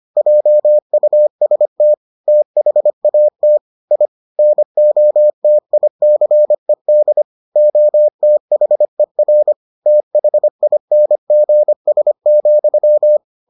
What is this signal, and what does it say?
Morse code, heard as a keyed tone, about 25 wpm, 600 Hz. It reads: JUST THAT I NOTICED OTHER THINGS,